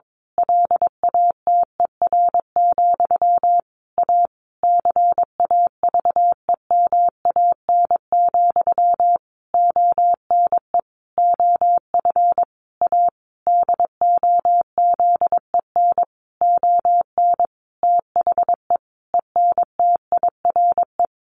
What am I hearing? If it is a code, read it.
Morse code, heard as a keyed tone, about 22 wpm, 700 Hz. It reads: LATER, A CAVEMAN, ONE OF A DOZEN ON THE ENTIRE
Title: A Sound Of Thunder 22wpm